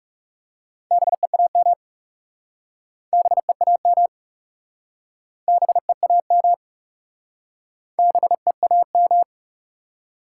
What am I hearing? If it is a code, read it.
Morse code, heard as a keyed tone, about 40 wpm, 700 Hz. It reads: BEAM BEAM BEAM BEAM